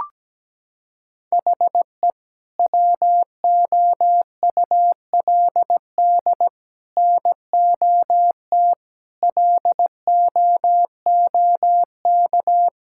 Morse code, heard as a keyed tone, about 17 wpm, 700 Hz. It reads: HE WOULD NOT LOOK
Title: Sentences from Top 1000 Words - Review Part 6 - 17wpm